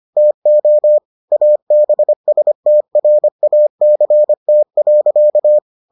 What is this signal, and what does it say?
Morse code, heard as a keyed tone, about 25 wpm, 600 Hz. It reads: TO ABSTRACT.